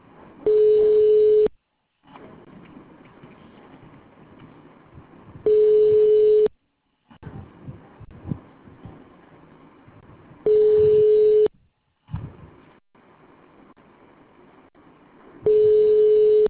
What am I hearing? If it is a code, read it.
Ringback tone (Europe, 425 Hz).